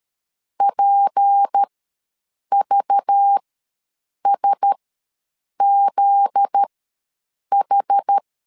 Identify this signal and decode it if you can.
Morse code, heard as a keyed tone, about 13 wpm, spaced as for 7 wpm, 800 Hz. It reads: PVSZH
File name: LEZIONI / 9 28 ws 12wpm 3354